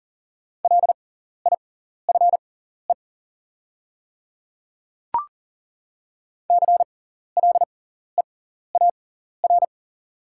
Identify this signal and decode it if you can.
Morse code, heard as a keyed tone, about 40 wpm, spaced as for 14 wpm, 700 Hz. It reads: LIFE CLEAR
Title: 6x Character Spacing - Top 500 Words - 40wpm